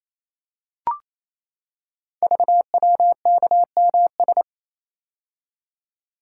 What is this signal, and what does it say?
Morse code, heard as a keyed tone, about 27 wpm, 700 Hz. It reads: VWKMS
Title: Code Groups - 5 Letters - 28wpm